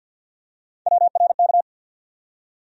Morse code, heard as a keyed tone, about 53 wpm, 700 Hz. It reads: WRK
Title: USA Call Sign Postfixes - 3 Letters 50wpm